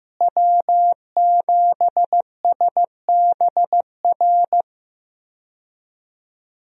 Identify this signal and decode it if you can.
Morse code, heard as a keyed tone, about 15 wpm, 700 Hz. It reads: W7SBR
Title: USA Call Signs - 5 Characters - Repeated 3x - 15wpm